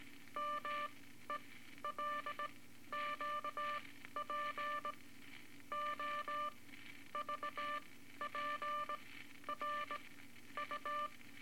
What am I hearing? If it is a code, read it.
Morse code, heard as a keyed tone, about 19 wpm, spaced as for 12 wpm, 1180 Hz, with noise. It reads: MELQP O VPR U
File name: Morse Code Records / 18 A LCM 164 Ten Words Per Minute